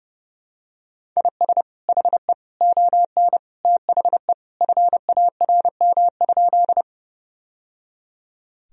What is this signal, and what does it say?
Morse code, heard as a keyed tone, about 30 wpm, 700 Hz. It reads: IS HE ON THE FARM?
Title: Sentences from Top 1000 Words - Review Part 2 - 30wpm